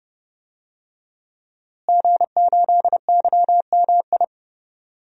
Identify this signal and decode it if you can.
Morse code, heard as a keyed tone, about 30 wpm, 700 Hz. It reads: G8YMI
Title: DXCC-100 Call Signs - 5 Characters - Repeated 3x - 30wpm